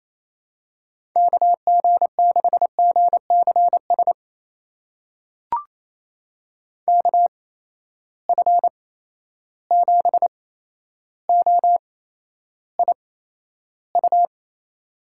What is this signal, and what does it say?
Morse code, heard as a keyed tone, about 28 wpm, spaced as for 12 wpm, 700 Hz. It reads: KG6GCS KF7OIU
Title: USA Call Signs - 6 Characters - 8x Character Spacing - 28wpm